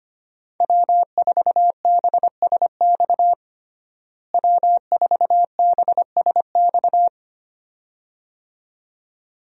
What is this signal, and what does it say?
Morse code, heard as a keyed tone, about 25 wpm, 700 Hz. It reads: W4BSX W4BSX
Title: USA Call Signs - 5 Characters - Repeated 2x - 25wpm